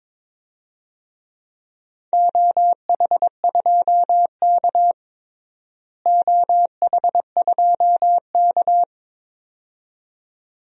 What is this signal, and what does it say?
Morse code, heard as a keyed tone, about 21 wpm, 700 Hz. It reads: OH2K OH2K
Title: DXCC-100 Call Signs - 4 Characters - Repeated 2x - 22wpm